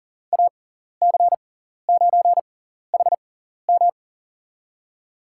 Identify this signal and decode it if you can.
Morse code, heard as a keyed tone, about 40 wpm, spaced as for 14 wpm, 700 Hz. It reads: AC9HM
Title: USA Call Signs - 5 Characters - 6x Character Spacing - 40wpm